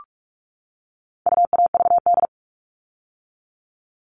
Morse code, heard as a keyed tone, about 44 wpm, 700 Hz. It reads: UAVD
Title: All Cut Numbers - 4-Digit 45wpm